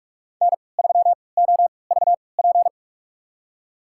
Morse code, heard as a keyed tone, about 46 wpm, spaced as for 25 wpm, 700 Hz. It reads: N3OVP
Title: USA Call Signs - 3x Character Spacing - 45wpm